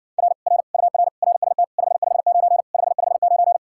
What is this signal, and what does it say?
Morse code, heard as a keyed tone, about 70 wpm, 700 Hz. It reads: R R UR RST 559 559